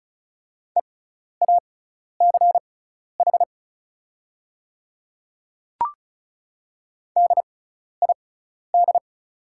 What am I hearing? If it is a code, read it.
Morse code, heard as a keyed tone, about 36 wpm, spaced as for 12 wpm, 700 Hz. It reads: EACH DID
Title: 6x Character Spacing - Top 100 Words - 35wpm